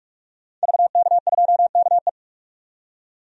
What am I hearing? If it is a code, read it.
Morse code, heard as a keyed tone, about 42 wpm, 700 Hz. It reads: VK2KE
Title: DXCC-100 Call Signs - 5 Characters - 45wpm